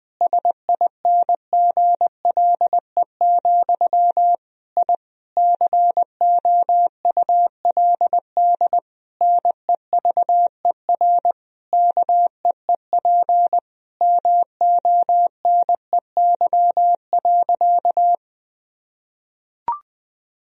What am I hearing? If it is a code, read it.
Morse code, heard as a keyed tone, about 19 wpm, 700 Hz. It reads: SINGLE, I COULD NEVER KEEP MONEY.
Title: Sentences from Top 1000 Words - Review Part 4 - 20wpm